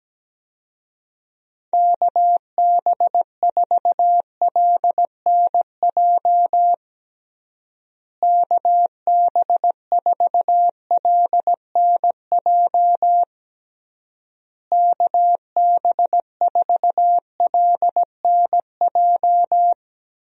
Morse code, heard as a keyed tone, about 17 wpm, 700 Hz. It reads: KB4LNJ KB4LNJ KB4LNJ
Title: USA Call Signs - 6 Characters - Repeated 3x - 17wpm